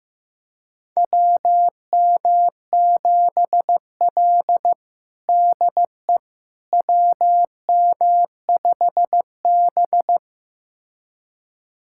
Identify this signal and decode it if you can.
Morse code, heard as a keyed tone, about 15 wpm, 700 Hz. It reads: WM7L DE WM5B
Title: USA Callsign DE Callsign - 4 Characters - 15wpm